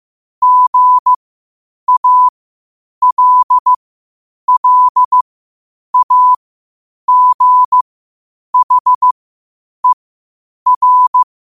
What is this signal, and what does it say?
Morse code, heard as a keyed tone, about 15 wpm, spaced as for 8 wpm, 1000 Hz. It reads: GALLAGHER
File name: qotdfiles / 8wpm